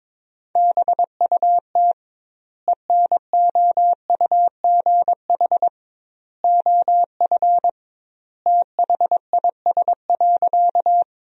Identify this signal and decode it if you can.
Morse code, heard as a keyed tone, about 22 wpm, 700 Hz. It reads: BUT ENOUGH OF THIS.
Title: 2x Word-Spacing - Sentences from Top 400 Words - 22wpm